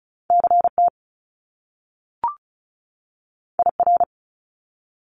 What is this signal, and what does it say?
Morse code, heard as a keyed tone, about 36 wpm, 700 Hz. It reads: CT IR
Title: Top Two Letter Combinations in Words 35wpm